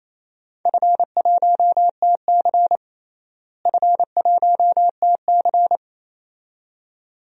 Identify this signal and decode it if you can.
Morse code, heard as a keyed tone, about 27 wpm, 700 Hz. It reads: F1TC F1TC
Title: DXCC-100 Call Signs - 4 Characters - Repeated 2x - 28wpm